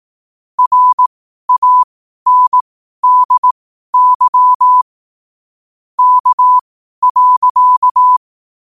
Morse code, heard as a keyed tone, about 18 wpm, spaced as for 12 wpm, 1000 Hz. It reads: RANDY K.
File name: qotdfiles / 12wpm